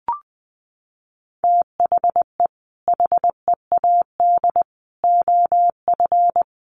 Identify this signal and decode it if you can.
Morse code, heard as a keyed tone, about 20 wpm, 700 Hz. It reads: THE HEAD OF
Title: Sets of 4 Words 20wpm